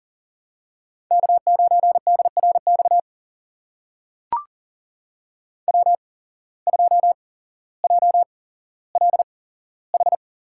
Morse code, heard as a keyed tone, about 40 wpm, spaced as for 17 wpm, 700 Hz. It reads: K9DRX W2JLH
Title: USA Call Signs - 5 Characters - 8x Character Spacing - 40wpm